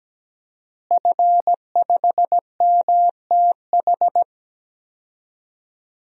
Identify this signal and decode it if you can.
Morse code, heard as a keyed tone, about 17 wpm, 700 Hz. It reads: F5MTH
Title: DXCC-100 Call Signs - 5 Characters - Repeated 2x - 17wpm